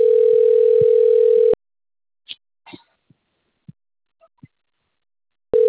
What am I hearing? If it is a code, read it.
Ringback tone (North America), 440 + 480 Hz.
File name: out- - - - - .41545